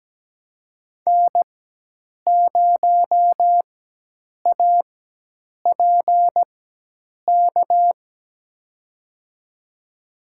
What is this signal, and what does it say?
Morse code, heard as a keyed tone, about 17 wpm, spaced as for 8 wpm, 700 Hz. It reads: N0APK
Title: USA Call Signs - 5 Characters - 4x Character Spacing - 17wpm